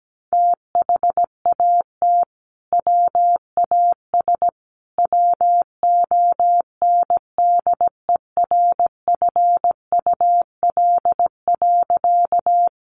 Morse code, heard as a keyed tone, about 17 wpm, 700 Hz. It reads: THAT WAS WONDERFUL.
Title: Sentences from Top 1100 Words 17wpm